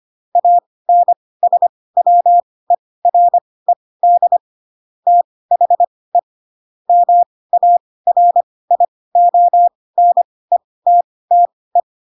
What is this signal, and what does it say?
Morse code, heard as a keyed tone, about 25 wpm, spaced as for 18 wpm, 700 Hz. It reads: ANSWERED THE MARIONETTE
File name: Pinocchio / Pinocchio ch7 12 25 ik1hge